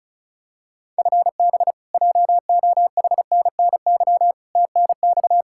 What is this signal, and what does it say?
Morse code, heard as a keyed tone, about 34 wpm, 700 Hz. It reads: FB JOHNNY TNX
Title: Practice QSO - Exchange 3 - More Info - 35wpm